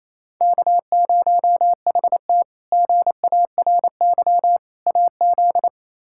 Morse code, heard as a keyed tone, about 28 wpm, 700 Hz. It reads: K0HT GARY AZ
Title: K1USN SST QSO Practice - 28wpm